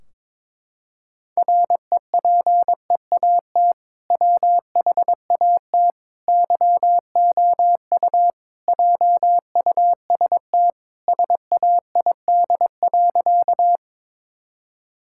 Morse code, heard as a keyed tone, about 22 wpm, 700 Hz. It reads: REPEAT WHAT YOU JUST SAID.
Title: Sentences from Top 1600 Words 22wpm